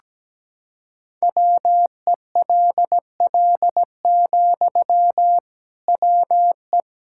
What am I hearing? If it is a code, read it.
Morse code, heard as a keyed tone, about 17 wpm, 700 Hz. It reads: WELL, WE
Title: Sentences from Top 1000 Words - Review Part 6 - 17wpm